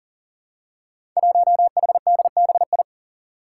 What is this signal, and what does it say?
Morse code, heard as a keyed tone, about 40 wpm, 700 Hz. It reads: 1HDBI
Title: Five Challenging Letter-Number Combinations 40wpm